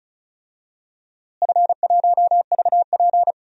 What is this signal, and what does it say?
Morse code, heard as a keyed tone, about 34 wpm, 700 Hz. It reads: F1VP
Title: DXCC-100 Call Signs - 4 Characters - Repeated 3x - 35wpm